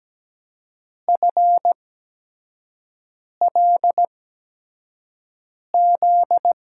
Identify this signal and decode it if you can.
Morse code, heard as a keyed tone, about 17 wpm, spaced as for 5 wpm, 700 Hz. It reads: FLZ
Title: USA Call Signs - 6 Characters - 8x Character Spacing - 17wpm